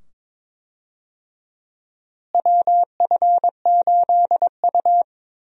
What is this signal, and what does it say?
Morse code, heard as a keyed tone, about 22 wpm, 700 Hz. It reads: WF8U
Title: USA Call Signs - 4 Characters - Repeated 2x - 22wpm